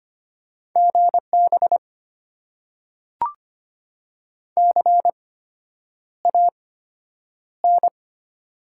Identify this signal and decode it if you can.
Morse code, heard as a keyed tone, about 25 wpm, spaced as for 9 wpm, 700 Hz. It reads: GB CAN
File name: Top 90 QSO Elements - 8x Character Spacing - 25wpm